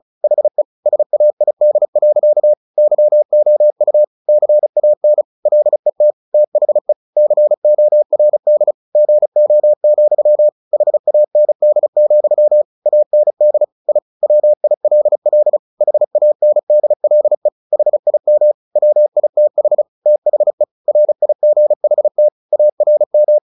Morse code, heard as a keyed tone, about 34 wpm, 600 Hz. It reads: HE SAID. YOU CAN LET THE CORD GO, HAND, AND I WILL HANDLE HIM WITH THE RIGHT ARM
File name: the old man and the sea / the old man and the sea 1 35wpm